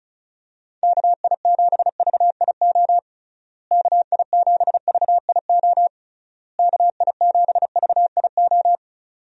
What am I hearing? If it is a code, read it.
Morse code, heard as a keyed tone, about 34 wpm, 700 Hz. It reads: KI7VIO KI7VIO KI7VIO
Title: USA Call Signs - 6 Characters - Repeated 3x - 35wpm